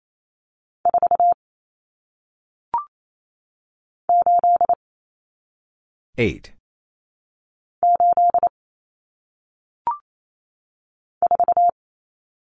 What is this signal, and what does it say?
Morse code, heard as a keyed tone, about 28 wpm, 700 Hz. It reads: 4 8 8 4
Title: Numbers - 1-Digit 28wpm